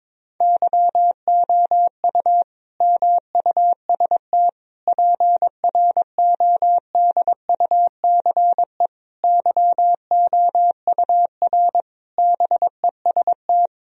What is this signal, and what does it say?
Morse code, heard as a keyed tone, about 21 wpm, 700 Hz. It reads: YOU MUST PRODUCE YOUR BEST
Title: Sentences from Top 1000 Words - Review Part 3 - 22wpm